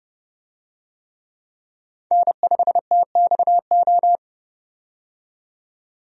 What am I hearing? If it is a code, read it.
Morse code, heard as a keyed tone, about 30 wpm, 700 Hz. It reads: N5TXO
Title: USA Call Signs - 5 Characters - 30wpm